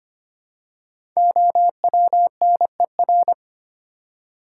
Morse code, heard as a keyed tone, about 25 wpm, 700 Hz. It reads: OWNER
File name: 5 Letter Words - First 3 Letters Given - 25wpm